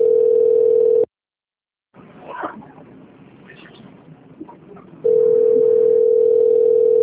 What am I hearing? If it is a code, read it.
Ringback tone (North America), 440 + 480 Hz.